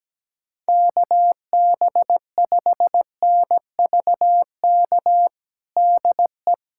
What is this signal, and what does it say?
Morse code, heard as a keyed tone, about 17 wpm, 700 Hz. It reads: KB5NVK DE
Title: USA Callsign DE Callsign 17wpm